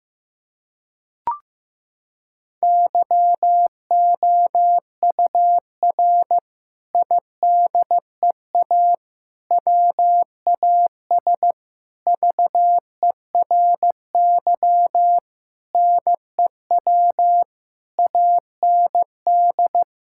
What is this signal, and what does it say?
Morse code, heard as a keyed tone, about 15 wpm, 700 Hz. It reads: YOUR IDEA WAS VERY NEW AND
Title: Sentences from Top 1000 Words - Review Part 6 - 15wpm